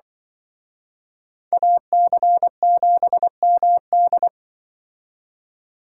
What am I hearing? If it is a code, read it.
Morse code, heard as a keyed tone, about 24 wpm, 700 Hz. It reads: AC7MD
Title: Speed Racing - USA 2x2 Call Signs 20wpm